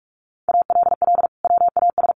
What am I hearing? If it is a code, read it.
Morse code, heard as a keyed tone, about 44 wpm, 700 Hz. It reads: ALL WAS
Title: Sentences from Top 800 Words 45wpm